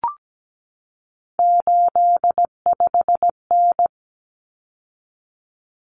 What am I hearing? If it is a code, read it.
Morse code, heard as a keyed tone, about 17 wpm, 700 Hz. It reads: 85N
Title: Common Cut Numbers - 3-Digit 17wpm